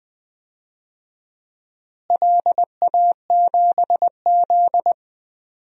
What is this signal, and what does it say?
Morse code, heard as a keyed tone, about 20 wpm, 700 Hz. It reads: LA7Z
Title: DXCC-100 Call Signs - 4 Characters - Repeated 3x - 20wpm